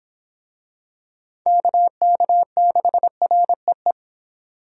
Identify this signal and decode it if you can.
Morse code, heard as a keyed tone, about 26 wpm, 700 Hz. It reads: KK6REE